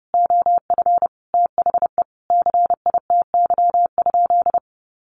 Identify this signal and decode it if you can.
Morse code, heard as a keyed tone, about 30 wpm, 700 Hz. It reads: OF THE CITY?
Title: Sentences from Top 1300 Words 30wpm